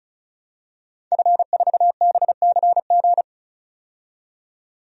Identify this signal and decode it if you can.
Morse code, heard as a keyed tone, about 36 wpm, 700 Hz. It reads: F4BCG
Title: DXCC-100 Call Signs - 5 Characters - 35wpm